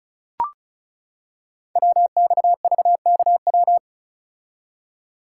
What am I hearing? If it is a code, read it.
Morse code, heard as a keyed tone, about 36 wpm, 700 Hz. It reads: WXVKW
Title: Code Groups - 5 Letters - 35wpm